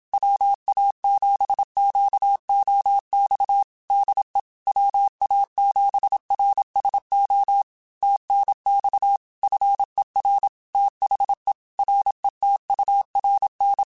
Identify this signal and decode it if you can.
Morse code, heard as a keyed tone, about 27 wpm, 800 Hz. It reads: WA7QOX DE WA7RSO TNX FER THE RETURN
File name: QSOs / QSOs-mp3s / WA7QOX 2@25